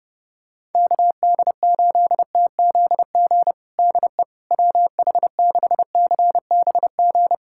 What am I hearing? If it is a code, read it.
Morse code, heard as a keyed tone, about 30 wpm, 690 Hz. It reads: KD8TZG DE WH6CBG
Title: USA Callsign DE Callsign - 6 Characters - 30wpm